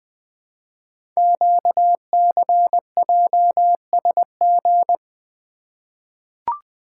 Morse code, heard as a keyed tone, about 20 wpm, 700 Hz. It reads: QCJSG
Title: Code Groups - 5 Letters - 20wpm